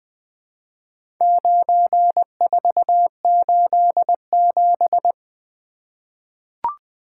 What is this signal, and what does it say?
Morse code, heard as a keyed tone, about 20 wpm, 700 Hz. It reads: 9487
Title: Four Letter-Number Combinations - 20wpm